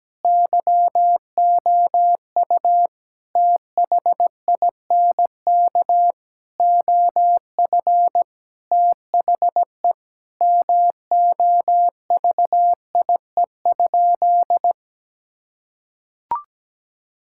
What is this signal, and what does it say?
Morse code, heard as a keyed tone, about 17 wpm, 700 Hz. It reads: YOU THINK OF THE MOVIE?
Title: Sentences from Top 1000 Words - Review Part 5 - 17wpm